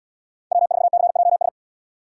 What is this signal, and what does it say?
Morse code, heard as a keyed tone, about 77 wpm, 700 Hz. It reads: W4CJS